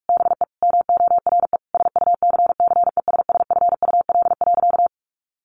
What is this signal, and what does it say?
Morse code, heard as a keyed tone, about 44 wpm, 700 Hz. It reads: BE MORE SUCCESSFUL.
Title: Sentences from Top 900 Words 45wpm